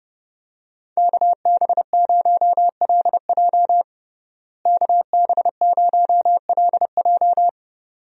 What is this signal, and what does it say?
Morse code, heard as a keyed tone, about 32 wpm, 700 Hz. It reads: KB0LJ KB0LJ
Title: USA Call Signs - 5 Characters - Repeated 2x - 30wpm